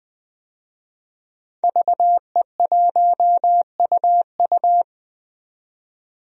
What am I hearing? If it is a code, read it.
Morse code, heard as a keyed tone, about 21 wpm, 700 Hz. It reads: VE1UU
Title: DXCC-100 Call Signs - 5 Characters - 20wpm